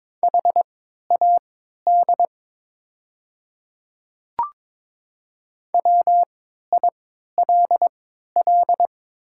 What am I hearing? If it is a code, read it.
Morse code, heard as a keyed tone, about 21 wpm, spaced as for 12 wpm, 700 Hz. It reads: HAD WILL
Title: 3x Character Spacing - Top 100 Words - 22wpm